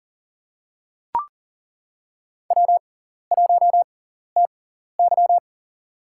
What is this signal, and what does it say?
Morse code, heard as a keyed tone, about 40 wpm, spaced as for 14 wpm, 700 Hz. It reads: W1TY
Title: USA Call Signs - 4 Characters - 6x Character Spacing - 40wpm